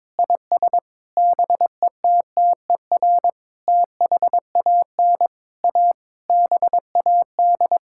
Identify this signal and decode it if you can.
Morse code, heard as a keyed tone, about 22 wpm, 700 Hz. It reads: IS BETTER THAN A BAD
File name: Sets of 9 Words - 22wpm